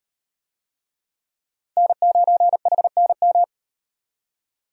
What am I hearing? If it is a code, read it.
Morse code, heard as a keyed tone, about 37 wpm, 700 Hz. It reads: N9HNM